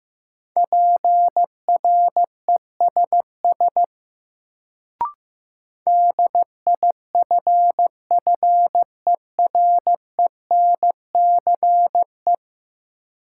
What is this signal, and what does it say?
Morse code, heard as a keyed tone, about 15 wpm, 700 Hz. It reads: PRESS DIFFERENCE
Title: Sentences from Top 800 Words - New Word Drill - 15wpm